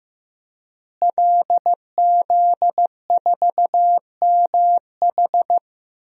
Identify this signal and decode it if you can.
Morse code, heard as a keyed tone, about 15 wpm, 700 Hz. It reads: LZ4MH
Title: DXCC-100 Call Signs - 5 Characters - Repeated 3x - 15wpm